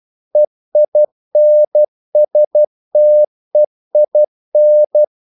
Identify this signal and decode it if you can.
Morse code, heard as a keyed tone, about 12 wpm, 600 Hz. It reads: EINSTEIN
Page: English quotations in CW (telegraphy - morse)